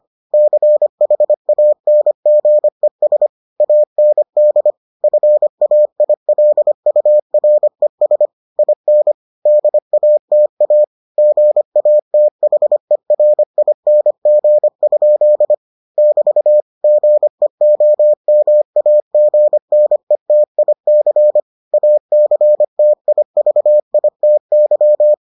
Morse code, heard as a keyed tone, about 25 wpm, 600 Hz. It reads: CHANGES AND FAILURES IN DATA GATHERING? <BT> GEOMAGNETIC ACTIVITY